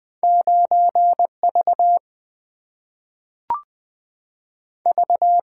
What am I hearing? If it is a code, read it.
Morse code, heard as a keyed tone, about 20 wpm, 700 Hz. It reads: 9V V
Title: Single-Letter Single-Number 20wpm